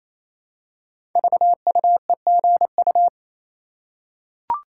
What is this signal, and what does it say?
Morse code, heard as a keyed tone, about 28 wpm, 700 Hz. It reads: VUEGU
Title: Code Groups - 5 Letters - 28wpm